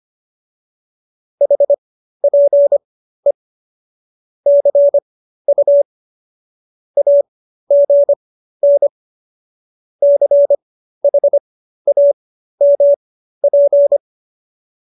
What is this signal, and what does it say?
Morse code, heard as a keyed tone, about 25 wpm, spaced as for 13 wpm, 570 Hz. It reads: HPE CU AGN CHAMP
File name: qso208b 13